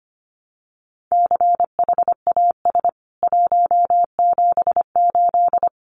Morse code, heard as a keyed tone, about 25 wpm, 700 Hz. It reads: CHAS 178
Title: CWT Exchange 25wpm